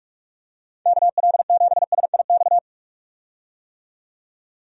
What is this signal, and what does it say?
Morse code, heard as a keyed tone, about 45 wpm, 700 Hz. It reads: KL7SIX
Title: DXCC-100 Call Signs - 6 Characters - 45wpm